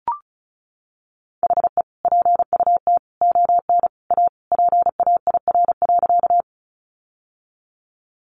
Morse code, heard as a keyed tone, about 34 wpm, 700 Hz. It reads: HE PUT ON A PAIR.
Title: Sentences from Top 1200 Words 35wpm